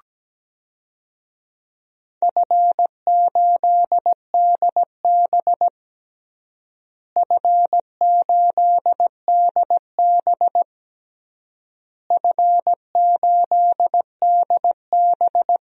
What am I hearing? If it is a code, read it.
Morse code, heard as a keyed tone, about 17 wpm, 700 Hz. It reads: F8DB F8DB F8DB
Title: DXCC-100 Call Signs - 4 Characters - Repeated 3x - 17wpm